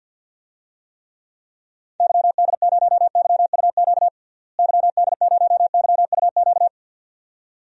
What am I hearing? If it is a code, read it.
Morse code, heard as a keyed tone, about 50 wpm, 700 Hz. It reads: YD0YUQ YD0YUQ
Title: DXCC-100 Call Signs - 6 Characters - Repeated 2x - 50wpm